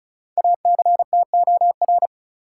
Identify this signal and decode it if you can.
Morse code, heard as a keyed tone, about 36 wpm, 700 Hz. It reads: ACTOR
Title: Sentences from Top 1500 Words - New Word Drill - 35wpm